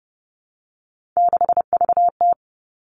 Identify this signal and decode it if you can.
Morse code, heard as a keyed tone, about 30 wpm, 700 Hz. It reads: 6VT
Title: Three Challenging Letter-Number Combinations 30wpm